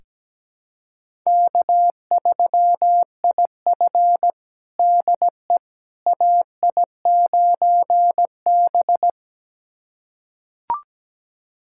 Morse code, heard as a keyed tone, about 17 wpm, 700 Hz. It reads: K3IF DE AI9B
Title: USA Callsign DE Callsign - 4 Characters - 17wpm